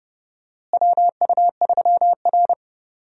Morse code, heard as a keyed tone, about 31 wpm, 700 Hz. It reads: WU3R
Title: USA Call Signs - 4 Characters - Repeated 2x - 30wpm